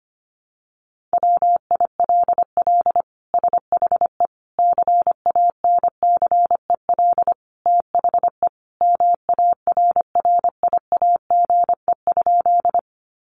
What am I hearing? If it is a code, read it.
Morse code, heard as a keyed tone, about 25 wpm, 700 Hz. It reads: WILL SHE CANCEL THE MARRIAGE?
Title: Sentences from Top 1200 Words 25wpm